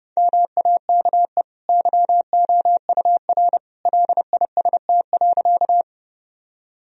Morse code, heard as a keyed tone, about 30 wpm, 700 Hz. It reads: MAKE YOUR LIST.
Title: Sentences from Top 1000 Words - Review Part 2 - 30wpm